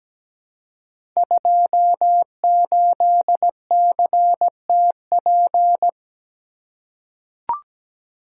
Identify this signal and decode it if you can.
Morse code, heard as a keyed tone, about 17 wpm, 700 Hz. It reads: 28CTP
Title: Code Groups - 5 Letter:Numbers - 17wpm